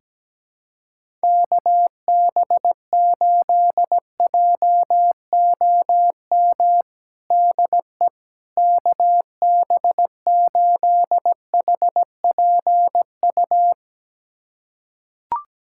Morse code, heard as a keyed tone, about 17 wpm, 700 Hz. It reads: KB8JOM DE KB8HPU
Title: USA Callsign DE Callsign - 6 Characters - 17wpm